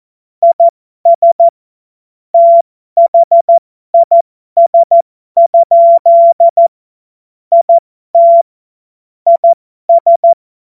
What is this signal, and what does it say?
Morse code, heard as a keyed tone, about 14 wpm, spaced as for 12 wpm, 680 Hz. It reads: IS THIS? IT IS
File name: Morsekurs / Witze